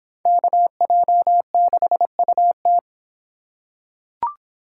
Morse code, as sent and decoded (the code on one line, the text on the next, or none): -.- .--- -.... ..- -
KJ6UT